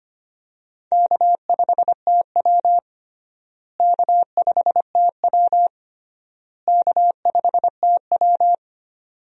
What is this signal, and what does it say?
Morse code, heard as a keyed tone, about 25 wpm, 700 Hz. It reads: K5TW K5TW K5TW
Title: USA Call Signs - 4 Characters - Repeated 3x - 25wpm